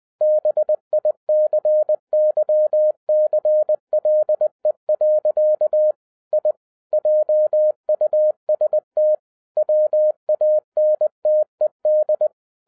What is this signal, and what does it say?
Morse code, heard as a keyed tone, about 20 wpm, 610 Hz. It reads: BICYCLE. I JUST WANTED
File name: 2012-01-mp3 / n05-CAME BACK TO HAM RADIO AFTER 30 YEARS OF INTERVAL-20wpm